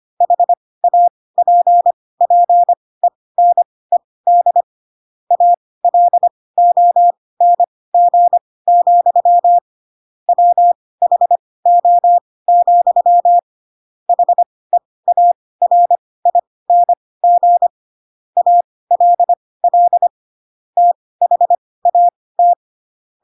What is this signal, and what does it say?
Morse code, heard as a keyed tone, about 25 wpm, spaced as for 18 wpm, 700 Hz. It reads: HAPPENED ALONG, WHO, HEARING ALL THAT
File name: Pinocchio ch3 12 25 ik1hge